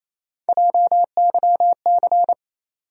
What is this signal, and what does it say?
Morse code, heard as a keyed tone, about 28 wpm, 700 Hz. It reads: JYC
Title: USA Call Sign Postfixes - 3 Letters 28wpm